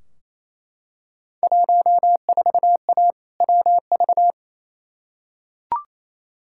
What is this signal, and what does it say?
Morse code, heard as a keyed tone, about 28 wpm, 700 Hz. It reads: 14A WV
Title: Field Day Exchanges Encore - 28wpm